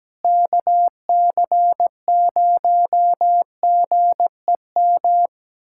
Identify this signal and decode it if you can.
Morse code, heard as a keyed tone, about 17 wpm, 700 Hz. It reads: KC0GEM
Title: USA Callsign DE Callsign 17wpm